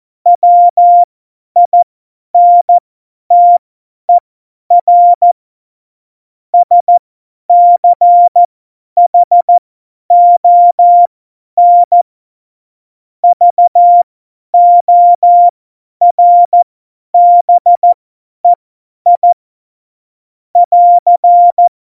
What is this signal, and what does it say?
Morse code, heard as a keyed tone, about 14 wpm, spaced as for 10 wpm, 700 Hz. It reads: WINTER SCHON VORBEI <AR>
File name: Morsekurs / Witze / Witz 22. Mai 2018, 50 F70